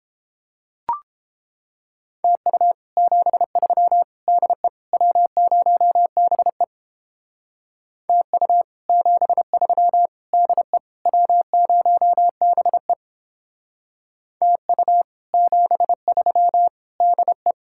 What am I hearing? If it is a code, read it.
Morse code, heard as a keyed tone, about 29 wpm, 700 Hz. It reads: TU 73 DE W0BE TU 73 DE W0BE TU 73 DE